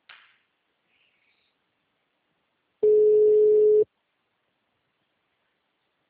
Ringback tone (Europe, 425 Hz).